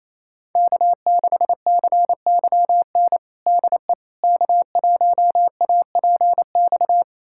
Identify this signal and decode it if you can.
Morse code, heard as a keyed tone, about 28 wpm, 690 Hz. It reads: K6CYN DE K1APX
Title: USA Callsign DE Callsign - 5 Characters - 28wpm